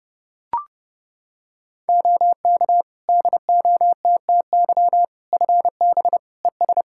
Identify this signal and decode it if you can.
Morse code, heard as a keyed tone, about 30 wpm, 700 Hz. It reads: OK DOTTY FB ES
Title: K1USN SST QSO Practice 30wpm